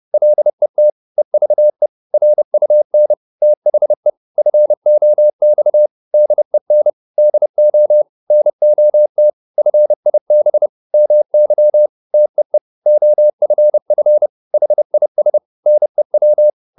Morse code, heard as a keyed tone, about 30 wpm, 600 Hz. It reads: LET EVE RUN THE FOX DEN DO NOT FIB MY TEE OFF HIS NEW